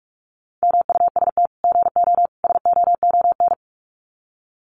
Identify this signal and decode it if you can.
Morse code, heard as a keyed tone, about 44 wpm, 700 Hz. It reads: MUST GO SOON
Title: Common QSO Phrases 45wpm